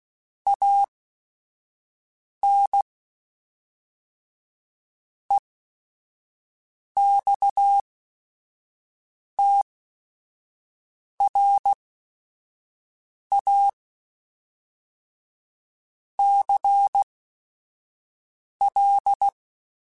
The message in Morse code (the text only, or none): AN EXTRA CL